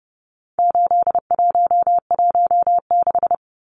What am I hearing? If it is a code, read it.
Morse code, heard as a keyed tone, about 30 wpm, 700 Hz. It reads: 8116
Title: Numbers - 4-Digit 30wpm